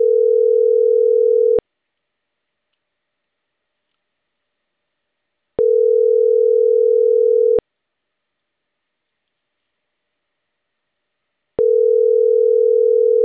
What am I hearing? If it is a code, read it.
Ringback tone (North America), 440 + 480 Hz.